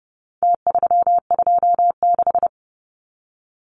Morse code, heard as a keyed tone, about 30 wpm, 700 Hz. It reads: T326
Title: Numbers - Contest Style - 4-Digit 30wpm